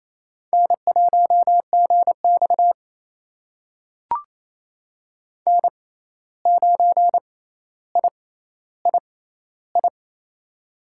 Morse code, heard as a keyed tone, about 27 wpm, spaced as for 13 wpm, 700 Hz. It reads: N1GX N9III
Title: USA Call Signs - 6x Character Spacing - 28wpm